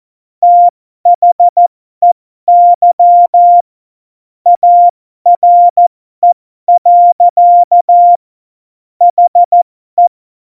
Morse code, heard as a keyed tone, about 14 wpm, spaced as for 12 wpm, 700 Hz. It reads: THEY ARE. HE
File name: Morsekurs / Witze